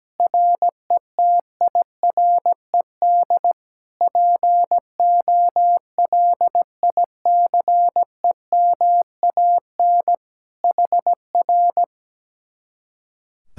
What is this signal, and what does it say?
Morse code, heard as a keyed tone, about 17 wpm, 700 Hz. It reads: RETIRED POLICEMAN HR
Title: Speed-Racing Common QSO Phrases Encore - 17wpm